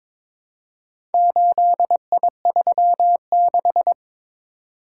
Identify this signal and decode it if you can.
Morse code, heard as a keyed tone, about 22 wpm, 700 Hz. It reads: 8I36
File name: practice / 22wpm / Four Letter-Number Combinations - 22wpm